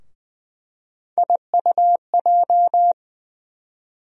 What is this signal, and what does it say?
Morse code, heard as a keyed tone, about 20 wpm, 700 Hz. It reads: IUJ
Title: USA Call Sign Postfixes - 3 Letters 20wpm